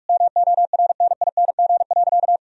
Morse code, heard as a keyed tone, about 46 wpm, 700 Hz. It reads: MORNING.
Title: Sentences from Top 1000 Words - Review Part 1 - 45wpm